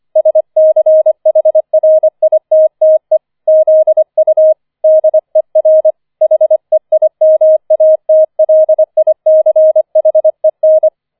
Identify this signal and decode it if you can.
Morse code, heard as a keyed tone, about 25 wpm, 630 Hz. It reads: SCHRITTE ZU DER HEIMATLICHEN
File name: 004 Vierter Streich 25wpm